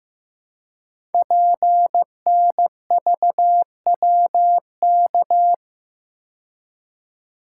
Morse code, heard as a keyed tone, about 15 wpm, 700 Hz. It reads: PNVWK
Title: Code Groups - 5 Letters - 15wpm